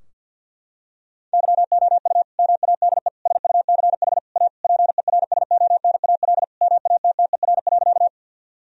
Morse code, heard as a keyed tone, about 50 wpm, 700 Hz. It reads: YOU MADE SUCH A PERSONAL MATTER.
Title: Sentences from Top 1000 Words - Review Part 4 - 50wpm